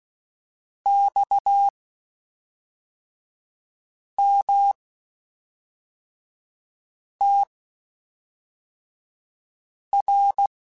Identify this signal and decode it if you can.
Morse code, heard as a keyed tone, about 16 wpm, spaced as for 3 wpm, 800 Hz. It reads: XMTR
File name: (spaced) / QSOs Spaced-mp3s / WA7QOX 1 @5 sp